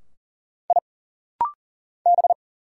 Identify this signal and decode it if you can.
Morse code, heard as a keyed tone, about 40 wpm, 700 Hz. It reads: I B
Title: Single Letters - Rapid-Fire 40wpm